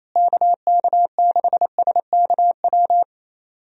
Morse code, heard as a keyed tone, about 28 wpm, 700 Hz. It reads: KK6SKW
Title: USA Call Signs - 6 Characters - Repeated 3x - 28wpm